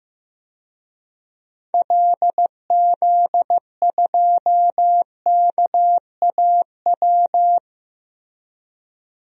Morse code, heard as a keyed tone, about 15 wpm, 700 Hz. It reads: LZ2KAW
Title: DXCC-100 Call Signs - 6 Characters - Repeated 2x - 15wpm